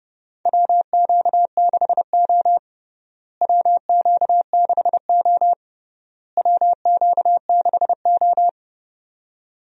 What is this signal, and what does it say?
Morse code, heard as a keyed tone, about 32 wpm, 700 Hz. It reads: WQ6O WQ6O WQ6O
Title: USA Call Signs - 4 Characters - Repeated 3x - 30wpm